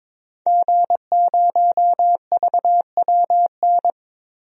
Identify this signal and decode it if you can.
Morse code, heard as a keyed tone, about 22 wpm, 700 Hz. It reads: G0VWN
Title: DXCC-100 Call Signs - 5 Characters - Repeated 2x - 22wpm